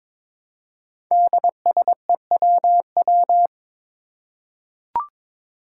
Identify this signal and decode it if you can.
Morse code, heard as a keyed tone, about 22 wpm, 700 Hz. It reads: DSEWW
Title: Five Challenging Letter-Number Combinations 22wpm